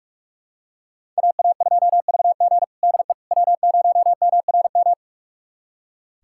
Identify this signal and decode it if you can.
Morse code, heard as a keyed tone, about 41 wpm, 700 Hz. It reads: AA2VG DE W0MRM
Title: USA Callsign DE Callsign - 5 Characters - 45wpm